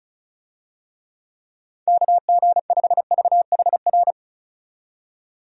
Morse code, heard as a keyed tone, about 34 wpm, 700 Hz. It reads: KG5VHR
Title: USA Call Signs - 6 Characters - 35wpm